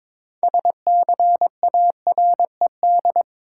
Morse code, heard as a keyed tone, about 22 wpm, 700 Hz. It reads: SCARED
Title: Sentences from Top 700 Words - New Word Drill - 22wpm